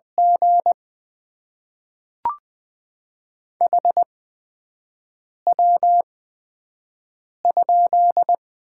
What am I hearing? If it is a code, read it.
Morse code, heard as a keyed tone, about 20 wpm, spaced as for 5 wpm, 700 Hz. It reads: G HW?
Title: Top 20 QSO Elements - 8x Character Spacing - 20wpm